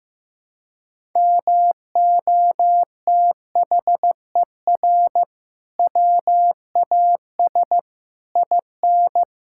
Morse code, heard as a keyed tone, about 15 wpm, 700 Hz. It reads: MOTHER WAS IN
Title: Sentences from Top 1000 Words - Review Part 6 - 15wpm